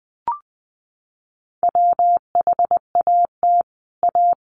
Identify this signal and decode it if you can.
Morse code, heard as a keyed tone, about 20 wpm, 700 Hz. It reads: WHAT A
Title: Sentences from Top 800 Words 20wpm